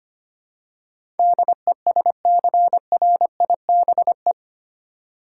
Morse code, heard as a keyed tone, about 25 wpm, 700 Hz. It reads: DESCRIBE
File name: Eight Letter Words - 25wpm